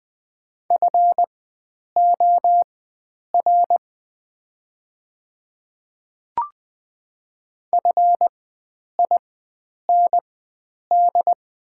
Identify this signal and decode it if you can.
Morse code, heard as a keyed tone, about 20 wpm, spaced as for 9 wpm, 700 Hz. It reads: FOR FIND
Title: 4x Character Spacing - Top 100 Words - 20wpm